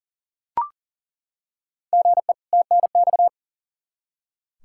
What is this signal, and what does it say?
Morse code, heard as a keyed tone, about 37 wpm, 700 Hz. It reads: GE TNX
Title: Practice QSO - Exchange 1 - Beginning - 40wpm